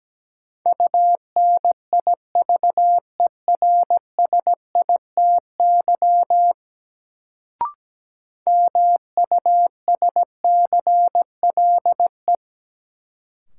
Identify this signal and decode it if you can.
Morse code, heard as a keyed tone, about 17 wpm, 700 Hz. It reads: UNIVERSITY MUSCLE
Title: Sentences from Top 1300 Words - New Word Drill Alt - 17wpm